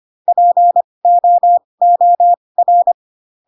Morse code, heard as a keyed tone, about 25 wpm, spaced as for 20 wpm, 700 Hz. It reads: POOR
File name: Pinocchio / Pinocchio ch6 15 25 ik1hge